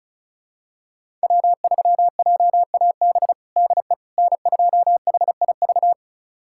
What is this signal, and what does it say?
Morse code, heard as a keyed tone, about 36 wpm, 700 Hz. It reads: W3JAB DE N2HIV
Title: USA Callsign DE Callsign - 5 Characters - 35wpm